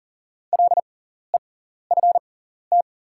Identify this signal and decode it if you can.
Morse code, heard as a keyed tone, about 40 wpm, spaced as for 14 wpm, 690 Hz. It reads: LEFT
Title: 6x Character Spacing - Top 500 Words - 40wpm